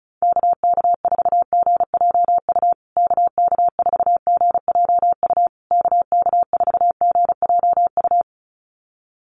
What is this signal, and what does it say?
Morse code, heard as a keyed tone, about 34 wpm, 700 Hz. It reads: KK4GJU KK4GJU KK4GJU
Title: USA Call Signs Repeated 3x 35wpm